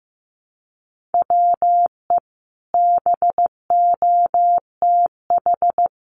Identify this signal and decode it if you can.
Morse code, heard as a keyed tone, about 15 wpm, 700 Hz. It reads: WE BOTH
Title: Sentences from Top 1200 Words 15wpm